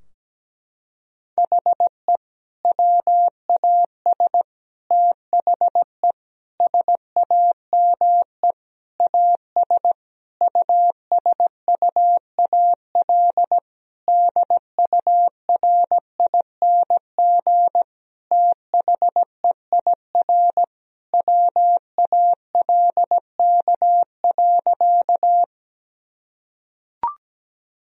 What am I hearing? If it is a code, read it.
Morse code, heard as a keyed tone, about 17 wpm, 700 Hz. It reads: HE WAS THE SAME AS USUAL DURING THEIR WALK.
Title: Sentences from Top 1000 Words - Review Part 2 - 17wpm